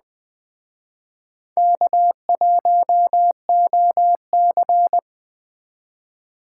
Morse code, heard as a keyed tone, about 20 wpm, 700 Hz. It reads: K1OC
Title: Speed Racing - USA 1x2 Call Signs 17wpm